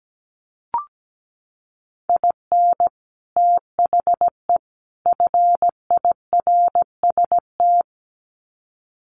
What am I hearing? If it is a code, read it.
Morse code, heard as a keyed tone, about 17 wpm, 700 Hz. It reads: IN THE FIRST
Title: Sets of 3 Words 17wpm